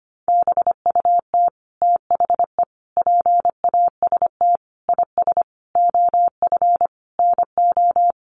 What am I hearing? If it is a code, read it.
Morse code, heard as a keyed tone, about 25 wpm, 700 Hz. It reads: BUT THE PAST IS OF NO
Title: Sentences from Top 1200 Words 25wpm